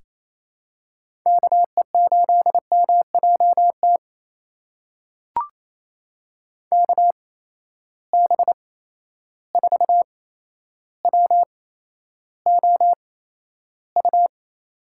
Morse code, heard as a keyed tone, about 28 wpm, spaced as for 12 wpm, 700 Hz. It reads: KE8MJT KB4WOU
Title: USA Call Signs - 6 Characters - 8x Character Spacing - 28wpm